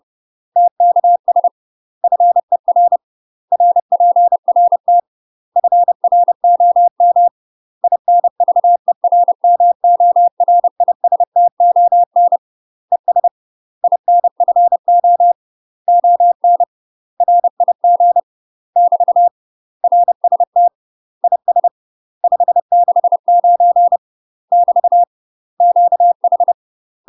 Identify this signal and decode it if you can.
Morse code, heard as a keyed tone, about 30 wpm, 700 Hz. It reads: TKS FER RPRT FROM INVERMORISTON ES INFO ON RIG <BT> RST IS 569 <BT> QH
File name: UK MASTER QSOs TRAINING / 30WPM / UK MASTER QSOs 0013